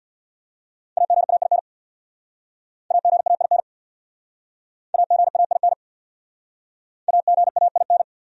Morse code, heard as a keyed tone, about 66 wpm, 700 Hz. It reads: AGAIN AGAIN AGAIN AGAIN